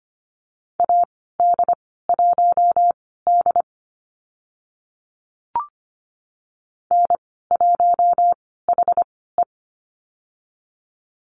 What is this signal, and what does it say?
Morse code, heard as a keyed tone, about 25 wpm, spaced as for 16 wpm, 690 Hz. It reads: AD1D N1HE
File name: Lesson 15g - Number 5 - Call Signs 25wpm